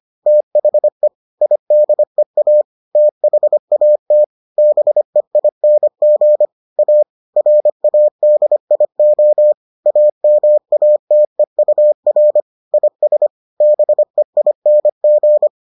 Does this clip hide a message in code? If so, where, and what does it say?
Yes: Morse code from 0.3 to 15.5 s, heard as a keyed tone, about 25 wpm, 600 Hz. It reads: THE IDEA THAT BEING A RADIO AMATEUR IS BEING